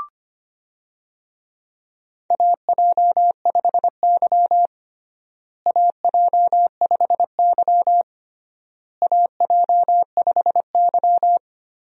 Morse code, heard as a keyed tone, about 25 wpm, 700 Hz. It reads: AJ5Y AJ5Y AJ5Y
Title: USA Call Signs - 4 Characters - Repeated 3x - 25wpm